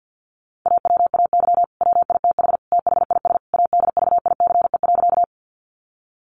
Morse code, heard as a keyed tone, about 50 wpm, 700 Hz. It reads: AWAY WITH THIS ADVICE.
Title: Sentences from Top 1100 Words 50wpm